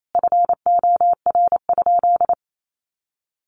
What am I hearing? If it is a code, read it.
Morse code, heard as a keyed tone, about 28 wpm, 700 Hz. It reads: FOR?
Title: Sentences from Top 1400 Words 28wpm